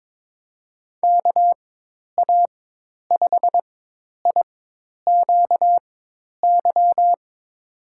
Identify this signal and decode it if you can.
Morse code, heard as a keyed tone, about 21 wpm, spaced as for 10 wpm, 700 Hz. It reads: KA5IQY
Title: USA Call Signs - 6 Characters - 4x Character Spacing - 22wpm